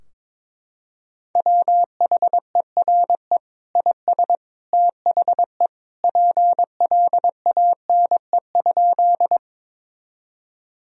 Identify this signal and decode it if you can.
Morse code, heard as a keyed tone, about 21 wpm, 700 Hz. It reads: WHERE IS THE PLANE?
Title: Sentences from Top 1000 Words - Review Part 3 - 22wpm